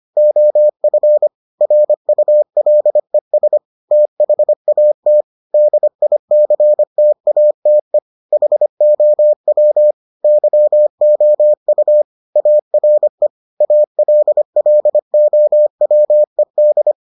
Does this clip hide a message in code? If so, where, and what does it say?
Yes: Morse code from 0.2 to 16.9 s, heard as a keyed tone, about 25 wpm, 600 Hz. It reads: OF RULES THAT DICTATE HOW YOU ARE ALLOWED